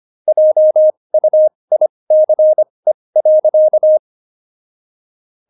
Morse code, heard as a keyed tone, about 25 wpm, spaced as for 20 wpm, 620 Hz. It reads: JUICE.
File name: Jokes-25-char-20-eff